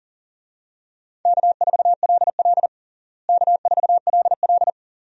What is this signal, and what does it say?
Morse code, heard as a keyed tone, about 40 wpm, 700 Hz. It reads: K4LL K4LL
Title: USA Call Signs - 4 Characters - Repeated 3x - 40wpm